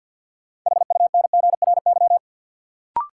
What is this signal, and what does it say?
Morse code, heard as a keyed tone, about 52 wpm, 700 Hz. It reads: HUNGRY
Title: Sentences from Top 1400 Words - New Word Drill - 50wpm